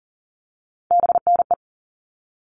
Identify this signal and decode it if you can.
Morse code, heard as a keyed tone, about 40 wpm, 700 Hz. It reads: BNE
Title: All Cut Numbers - 3-Digit 40wpm